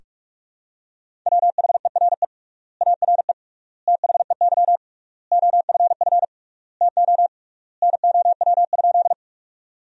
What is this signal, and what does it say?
Morse code, heard as a keyed tone, about 46 wpm, 700 Hz. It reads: WHERE ARE THEY OFF TO NOW?
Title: 3x Word-Spacing - Sentences from Top 200 Words - 45wpm